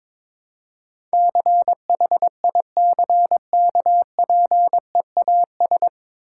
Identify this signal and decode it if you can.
Morse code, heard as a keyed tone, about 22 wpm, 690 Hz. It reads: CHICKPEAS
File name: Difficult Words with Hints - 22wpm